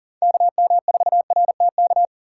Morse code, heard as a keyed tone, about 39 wpm, 700 Hz. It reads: KM4RTK
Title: USA Call Signs - 3x Character Spacing - 40wpm